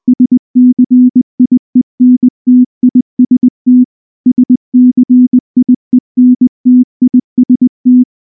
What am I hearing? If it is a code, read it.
Morse code, heard as a keyed tone, about 19 wpm, 250 Hz. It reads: SCIENTIST SCIENTIST
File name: morse / morse-training-20WPM-6